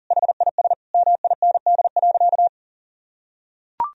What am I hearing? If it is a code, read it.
Morse code, heard as a keyed tone, about 40 wpm, 700 Hz. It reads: HIS MIND.
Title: Sentences from Top 1000 Words - Review Part 2 - 40wpm